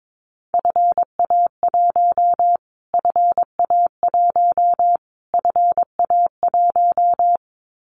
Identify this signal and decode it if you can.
Morse code, heard as a keyed tone, about 22 wpm, 700 Hz. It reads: FA1 FA1 FA1
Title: Callsign Prefixes for Top 25 DXCC - Repeated 3x - 22wpm